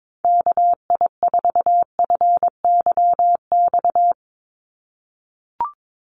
Morse code, heard as a keyed tone, about 22 wpm, 700 Hz. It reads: KI4FYX
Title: USA Call Signs 22wpm